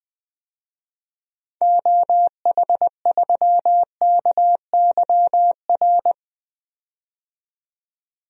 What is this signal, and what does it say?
Morse code, heard as a keyed tone, about 20 wpm, 700 Hz. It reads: OH3KYR
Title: DXCC-100 Call Signs - 6 Characters - 20wpm